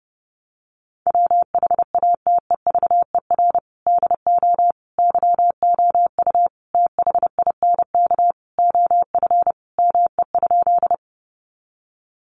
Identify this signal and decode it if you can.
Morse code, heard as a keyed tone, about 30 wpm, 700 Hz. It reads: WHATEVER DO YOU THINK OF ME?
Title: Sentences from Top 1100 Words 30wpm